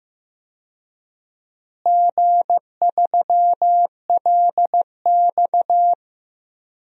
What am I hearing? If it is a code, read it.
Morse code, heard as a keyed tone, about 15 wpm, 700 Hz. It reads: G3LX
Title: DXCC-100 Call Signs - 4 Characters - Repeated 3x - 15wpm